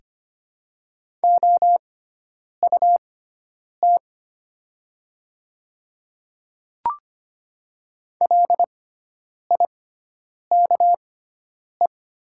Morse code, heard as a keyed tone, about 25 wpm, spaced as for 9 wpm, 700 Hz. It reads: OUT LIKE
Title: 6x Character Spacing - Top 300 Words - 25wpm